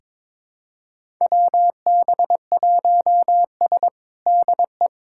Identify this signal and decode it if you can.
Morse code, heard as a keyed tone, about 22 wpm, 700 Hz. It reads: WB1S DE
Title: USA Callsign DE Callsign - 4 Characters - 22wpm